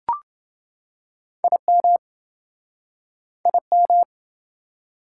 Morse code, heard as a keyed tone, about 29 wpm, 700 Hz. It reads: IM IM